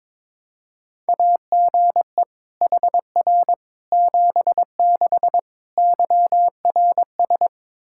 Morse code, heard as a keyed tone, about 22 wpm, 700 Hz. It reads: AGE HR 76 YRS
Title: Practice QSO - Exchange 3 - More Info - 22wpm